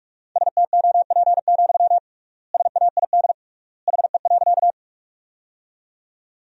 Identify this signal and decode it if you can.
Morse code, heard as a keyed tone, about 46 wpm, 700 Hz. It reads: STOP, SAID HE.
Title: 3x Word-Spacing - Sentences from Top 300 Words - 45wpm